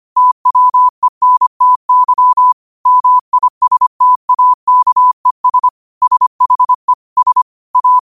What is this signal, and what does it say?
Morse code, heard as a keyed tone, about 25 wpm, 1000 Hz. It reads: TWENTY MISTAKES SHE S A
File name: qotdfiles / 25wpm